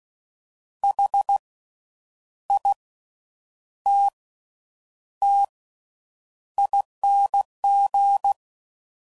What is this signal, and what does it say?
Morse code, heard as a keyed tone, about 16 wpm, spaced as for 8 wpm, 800 Hz. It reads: HITTING
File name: Suffixes / Special INGs / Slow With Faster Ings